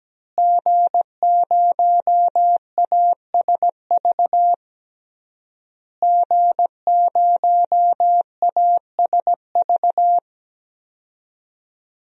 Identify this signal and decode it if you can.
Morse code, heard as a keyed tone, about 17 wpm, 700 Hz. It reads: G0ASV G0ASV
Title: DXCC-100 Call Signs - 5 Characters - Repeated 3x - 17wpm